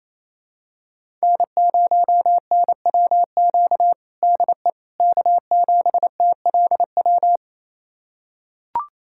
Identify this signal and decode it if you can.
Morse code, heard as a keyed tone, about 28 wpm, 700 Hz. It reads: N0NWQ DE K7TLW
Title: USA Callsign DE Callsign - 5 Characters - 28wpm